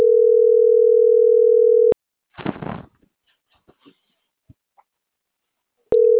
Ringback tone (North America), 440 + 480 Hz.